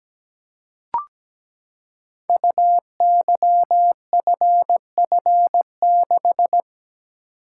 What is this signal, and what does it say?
Morse code, heard as a keyed tone, about 17 wpm, 700 Hz. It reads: UYFF6
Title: Code Groups - 5 Letter:Numbers - 17wpm